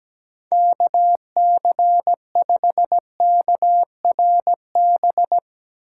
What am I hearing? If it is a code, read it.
Morse code, heard as a keyed tone, about 17 wpm, 700 Hz. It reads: KC5KRB
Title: USA Call Signs - 4x Character Spacing - 17wpm